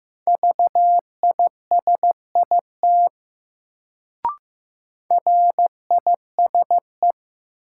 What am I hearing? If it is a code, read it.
Morse code, heard as a keyed tone, about 15 wpm, 700 Hz. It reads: VISIT RISE
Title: Sentences from Top 900 Words - New Word Drill - 15wpm